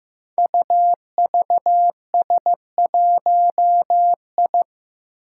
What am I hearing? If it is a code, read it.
Morse code, heard as a keyed tone, about 15 wpm, 700 Hz. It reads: UVS1I
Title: Five Challenging Letter-Number Combinations 15wpm